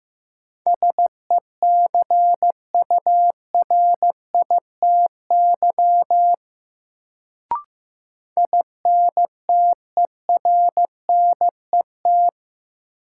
Morse code, heard as a keyed tone, about 15 wpm, 700 Hz. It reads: SECURITY INTERNET
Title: Sentences from Top 1000 Words - Review Part 4 - New Word Drill - 15wpm